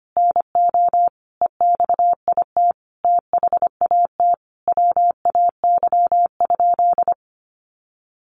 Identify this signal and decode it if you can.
Morse code, heard as a keyed tone, about 25 wpm, 700 Hz. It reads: NO EXIT THAT WAY?
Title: Sentences from Top 1100 Words 25wpm